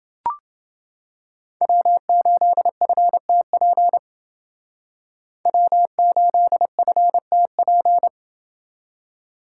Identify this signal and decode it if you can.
Morse code, heard as a keyed tone, about 29 wpm, 700 Hz. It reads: W8FTP W8FTP